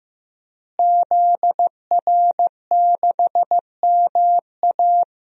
Morse code, heard as a keyed tone, about 14 wpm, 700 Hz. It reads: ZR6MA
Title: DXCC-100 Call Signs - 5 Characters - Repeated 3x - 15wpm